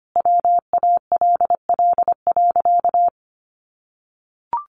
Morse code, heard as a keyed tone, about 25 wpm, 700 Hz. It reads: WALL.
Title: Sentences from Top 1100 Words 25wpm